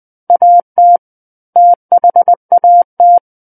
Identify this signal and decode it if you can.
Morse code, heard as a keyed tone, about 20 wpm, 700 Hz. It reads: AT THAT
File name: 20 WPM